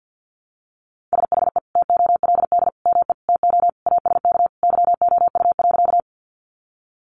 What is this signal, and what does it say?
Morse code, heard as a keyed tone, about 50 wpm, 700 Hz. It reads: SHE TOLD ME TO ASK YOU.
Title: Sentences from Top 400 Words 50wpm